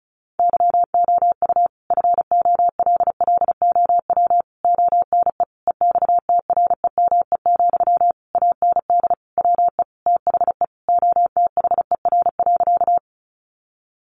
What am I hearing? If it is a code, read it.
Morse code, heard as a keyed tone, about 34 wpm, 700 Hz. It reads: YOU FOLLOW ONE EXTREME, AND WE THE OTHER.
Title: Sentences from Top 1300 Words 35wpm